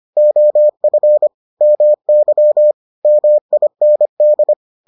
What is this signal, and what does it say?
Morse code, heard as a keyed tone, about 25 wpm, 600 Hz. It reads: OF MY MIND